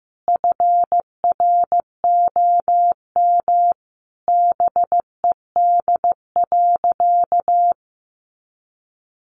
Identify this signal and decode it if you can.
Morse code, heard as a keyed tone, about 15 wpm, 700 Hz. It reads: FROM BED.
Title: Sentences from Top 900 Words 15wpm